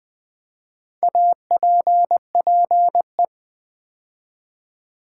Morse code, heard as a keyed tone, about 20 wpm, 700 Hz. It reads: APPE
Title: Build A Word - 2 Letters At A Time - 6-Letter Words - 20wpm